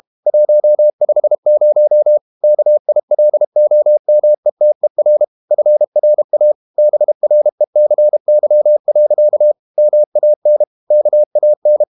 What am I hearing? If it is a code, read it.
Morse code, heard as a keyed tone, about 32 wpm, 600 Hz. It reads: 150 KILOMETER FRA BRECY. MAN KAN
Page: Ferie 32 wpm